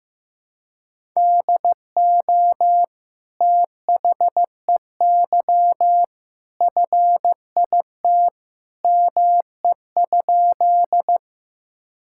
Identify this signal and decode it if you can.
Morse code, heard as a keyed tone, about 15 wpm, 700 Hz. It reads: DO THEY FIT ME?
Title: Sentences from Top 1000 Words - Review Part 5 - 15wpm